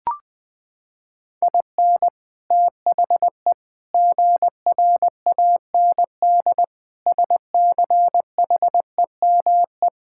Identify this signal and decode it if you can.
Morse code, heard as a keyed tone, about 20 wpm, 700 Hz. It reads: IN THE GRAND SCHEME
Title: Sentences with 6-Word Phrases - 20wpm